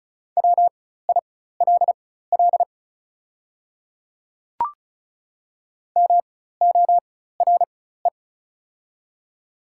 Morse code, heard as a keyed tone, about 36 wpm, spaced as for 16 wpm, 700 Hz. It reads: WILL MORE
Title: 4x Character Spacing - Top 200 Words - 35wpm